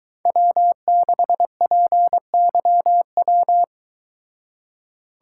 Morse code, heard as a keyed tone, about 22 wpm, 700 Hz. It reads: W6PYW